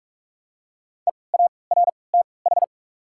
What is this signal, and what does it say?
Morse code, heard as a keyed tone, about 44 wpm, spaced as for 25 wpm, 700 Hz. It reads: EARTH
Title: 3x Character Spacing - Top 300 Words - 45wpm